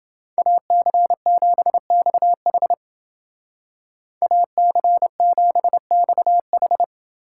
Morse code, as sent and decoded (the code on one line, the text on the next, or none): .- -.-. --... -..- .... / .- -.-. --... -..- ....
AC7XH AC7XH